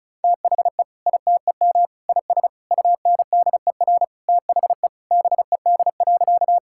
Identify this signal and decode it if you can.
Morse code, heard as a keyed tone, about 36 wpm, 700 Hz. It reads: THE ITEM IS UNDER THE BED.
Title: Sentences from Top 1000 Words - Review Part 6 - 35wpm